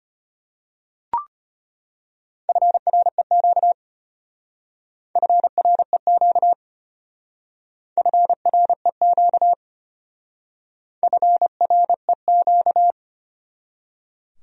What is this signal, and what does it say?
Morse code, heard as a keyed tone, about 33 wpm, 700 Hz. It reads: FREQ FREQ FREQ FREQ